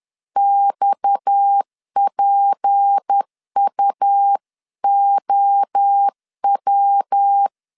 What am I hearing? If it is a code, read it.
Morse code, heard as a keyed tone, about 10 wpm, 800 Hz. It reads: XPUOW